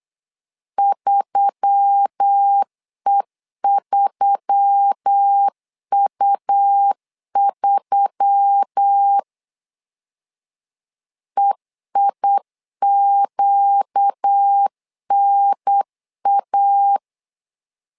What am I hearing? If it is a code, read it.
Morse code, heard as a keyed tone, about 8 wpm, 800 Hz. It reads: 3E3U3 EIQNA
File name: LEZIONI / 4 Q1S3 8wpm 3037